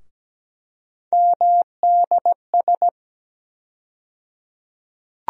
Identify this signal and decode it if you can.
Morse code, heard as a keyed tone, about 17 wpm, 700 Hz. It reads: MDS
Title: USA Call Sign Postfixes - 3 Letters 17wpm